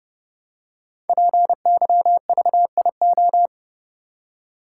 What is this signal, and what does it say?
Morse code, heard as a keyed tone, about 29 wpm, 700 Hz. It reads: PYVIO
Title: Code Groups - 5 Letter:Numbers - 30wpm